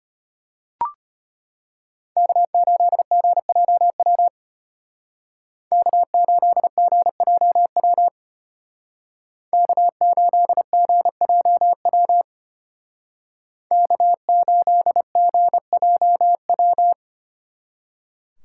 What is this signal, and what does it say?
Morse code, heard as a keyed tone, about 30 wpm, 700 Hz. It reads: K8GJW K8GJW K8GJW K8GJW
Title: Speed Racing - USA 1x3 Call Signs 25wpm